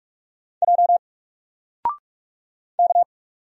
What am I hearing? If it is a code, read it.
Morse code, heard as a keyed tone, about 46 wpm, 700 Hz. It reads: J K
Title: Single Character (Letter:Number:Punctuation:Prosign) Alt 45wpm